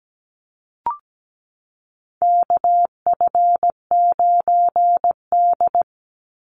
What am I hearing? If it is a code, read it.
Morse code, heard as a keyed tone, about 17 wpm, 700 Hz. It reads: KF9D
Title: Call Signs Repeated 2x - 17wpm